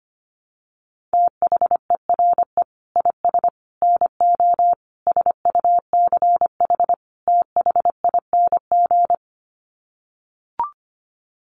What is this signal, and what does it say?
Morse code, heard as a keyed tone, about 25 wpm, 700 Hz. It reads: THERE IS NO SUCH THING
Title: Sets of 5 Words Encore 25wpm